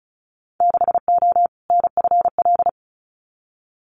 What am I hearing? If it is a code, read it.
Morse code, heard as a keyed tone, about 34 wpm, 700 Hz. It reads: 6O NFL
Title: Winter Field Day Exchanges 35wpm